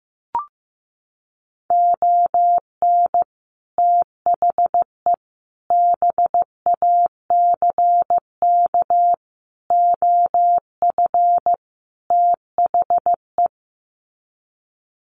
Morse code, heard as a keyed tone, about 15 wpm, 700 Hz. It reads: ON THE BACK OF THE
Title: Sets of 5 Words 15wpm